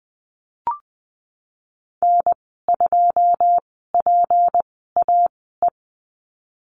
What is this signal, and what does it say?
Morse code, heard as a keyed tone, about 20 wpm, spaced as for 15 wpm, 700 Hz. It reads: N2PAE
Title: 179 Lesson 29g - Slash - Call Signs 20wpm